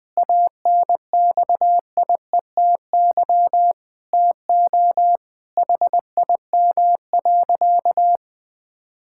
Morse code, heard as a keyed tone, about 20 wpm, 700 Hz. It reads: ANXIETY TO HIM.
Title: Sentences from Top 1600 Words 20wpm